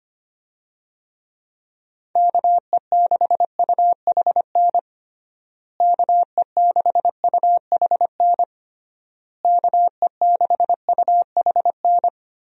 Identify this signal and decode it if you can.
Morse code, heard as a keyed tone, about 25 wpm, 700 Hz. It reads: KE6UHN KE6UHN KE6UHN
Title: USA Call Signs - 6 Characters - Repeated 3x - 25wpm